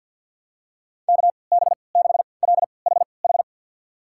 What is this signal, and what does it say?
Morse code, heard as a keyed tone, about 46 wpm, spaced as for 27 wpm, 700 Hz. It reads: KB6LHH
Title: USA Call Signs - 3x Character Spacing - 50wpm